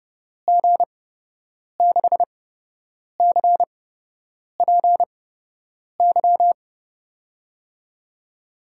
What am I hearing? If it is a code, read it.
Morse code, heard as a keyed tone, about 31 wpm, spaced as for 8 wpm, 700 Hz. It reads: G6CPY
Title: USA Call Signs - 6 Characters - 8x Character Spacing - 30wpm